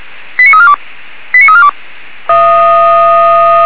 Five-tone selective call (ZVEI), heard as quick runs of five tones, tones of about 80 ms. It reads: 89342 89342